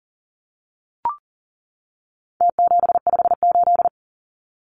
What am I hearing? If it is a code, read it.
Morse code, heard as a keyed tone, about 40 wpm, 700 Hz. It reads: T758
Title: Numbers - Contest Style - 4-Digit 40wpm